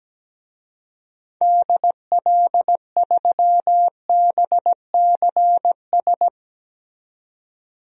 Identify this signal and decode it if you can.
Morse code, heard as a keyed tone, about 17 wpm, 700 Hz. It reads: DL3BCS
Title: DXCC-100 Call Signs - 6 Characters - 17wpm